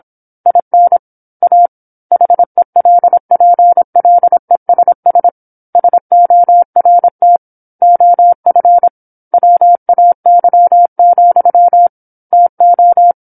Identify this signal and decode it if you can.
Morse code, heard as a keyed tone, about 25 wpm, 700 Hz. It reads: IN A HELPLESS SORT OF WAY, TO
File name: 08-Chapter-08 26 WPM